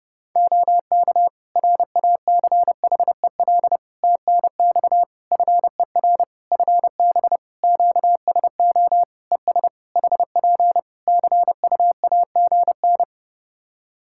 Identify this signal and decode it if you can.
Morse code, heard as a keyed tone, about 32 wpm, 700 Hz. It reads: OK RACHEL TNX FER FB QSO ES HP CUAGN
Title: K1USN SST QSO Practice 30wpm